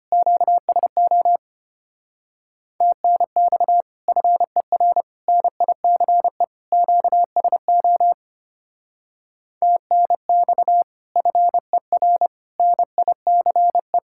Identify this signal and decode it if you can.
Morse code, heard as a keyed tone, about 30 wpm, 700 Hz. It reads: QSO TNX FER NICE QSO TNX FER NICE
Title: Speed-Racing Common QSO Phrases - 25wpm